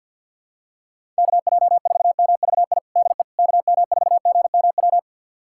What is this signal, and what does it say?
Morse code, heard as a keyed tone, about 47 wpm, 700 Hz. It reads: KJ4MVI DE KM4GMW
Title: USA Callsign DE Callsign - 6 Characters - 50wpm